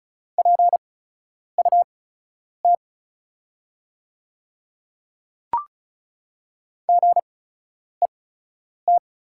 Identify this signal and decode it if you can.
Morse code, heard as a keyed tone, about 36 wpm, spaced as for 10 wpm, 690 Hz. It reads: PUT GET
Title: 8x Character Spacing - Top 200 Words - 35wpm